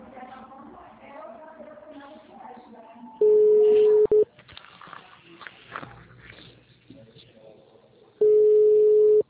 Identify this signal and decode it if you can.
Ringback tone (Europe, 425 Hz), with speech.
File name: out- -2044- - - .162